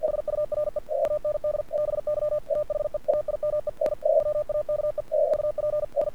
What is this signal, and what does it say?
Morse code, heard as a keyed tone, about 48 wpm, 1200 Hz. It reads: SURE AND BY THE TIME I WAKE UP I
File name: archived / CRSnet